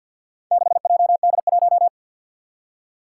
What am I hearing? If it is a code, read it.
Morse code, heard as a keyed tone, about 53 wpm, 700 Hz. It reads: 6JD1
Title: Four Challenging Letter-Number Combinations 50wpm